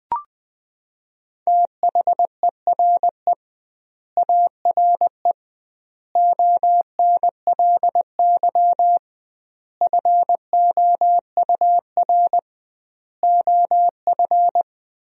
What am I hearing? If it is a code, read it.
Morse code, heard as a keyed tone, about 21 wpm, 700 Hz. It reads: THERE ARE ONLY FOUR OF
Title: 2x Word-Spacing - Sentences from Top 300 Words - 20wpm